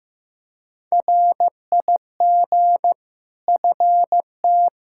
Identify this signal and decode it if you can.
Morse code, heard as a keyed tone, about 15 wpm, 700 Hz. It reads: RIG FT
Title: Practice QSO - Exchange 2 - Info - 15wpm